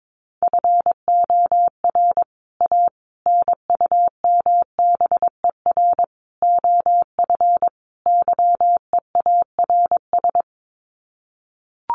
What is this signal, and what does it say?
Morse code, heard as a keyed tone, about 22 wpm, 700 Hz. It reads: FOR A NUMBER OF YEARS
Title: Sets of 5 Words Encore 22wpm